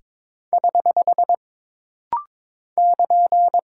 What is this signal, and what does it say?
Morse code, heard as a keyed tone, about 22 wpm, 700 Hz. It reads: <HH> <KN>
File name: practice / 22wpm / Single Punctuation-Prosign - Rapid-Fire - 22wpm